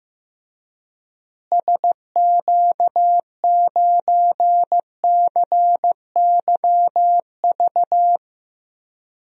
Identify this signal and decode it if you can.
Morse code, heard as a keyed tone, about 15 wpm, 700 Hz. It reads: SQ9CYV
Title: DXCC-100 Call Signs - 6 Characters - Repeated 3x - 15wpm